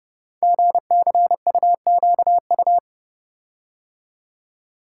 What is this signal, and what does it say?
Morse code, heard as a keyed tone, about 30 wpm, 700 Hz. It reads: GCUQU
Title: Code Groups - 5 Letters - 30wpm